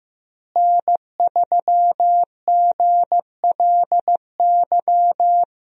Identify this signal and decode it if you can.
Morse code, heard as a keyed tone, about 15 wpm, 700 Hz. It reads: N3GLY
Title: USA Call Signs - 5 Characters - Repeated 3x - 15wpm